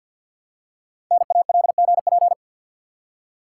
Morse code, heard as a keyed tone, about 52 wpm, 700 Hz. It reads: NALGP
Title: Code Groups - 5 Letters - 50wpm